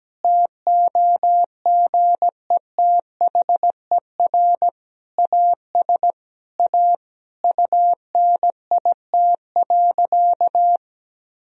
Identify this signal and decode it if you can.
Morse code, heard as a keyed tone, about 17 wpm, 700 Hz. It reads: TOGETHER AS A UNIT.
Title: Sentences from Top 1000 Words - Review Part 2 - 17wpm